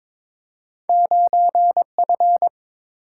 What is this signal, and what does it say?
Morse code, heard as a keyed tone, about 21 wpm, 700 Hz. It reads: 9F
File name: practice / 22wpm / Two Letter-Number Combinations - 22wpm